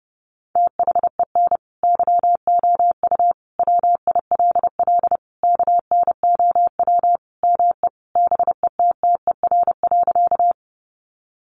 Morse code, heard as a keyed tone, about 30 wpm, 700 Hz. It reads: THEN YOU WILL KNOW ME BETTER.
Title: Sentences from Top 400 Words 30wpm